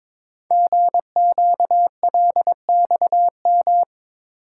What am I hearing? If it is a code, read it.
Morse code, heard as a keyed tone, about 22 wpm, 700 Hz. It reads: GQLXM
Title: Code Groups - 5 Letters - 22wpm